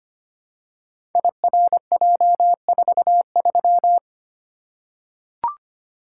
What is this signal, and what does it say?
Morse code, heard as a keyed tone, about 25 wpm, 700 Hz. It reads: IRJ43
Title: Code Groups - 5 Letter:Numbers - 25wpm